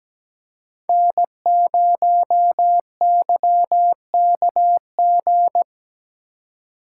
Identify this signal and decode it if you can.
Morse code, heard as a keyed tone, about 20 wpm, 700 Hz. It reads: N0YKG
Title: Speed Racing - USA 1x3 Call Signs 17wpm